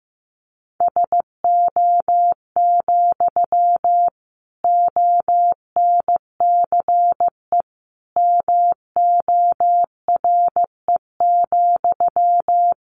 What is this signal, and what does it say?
Morse code, heard as a keyed tone, about 15 wpm, 700 Hz. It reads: SO, ONCE MORE,
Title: Sentences from Top 300 Words 15wpm